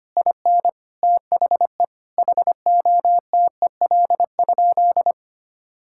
Morse code, heard as a keyed tone, about 25 wpm, 700 Hz. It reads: IN THE HOTEL?
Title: Sentences from Top 1600 Words 25wpm